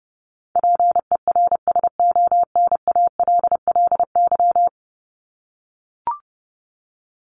Morse code, heard as a keyed tone, about 30 wpm, 700 Hz. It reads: PERSONALLY
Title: Top 1251-1500 Words 30wpm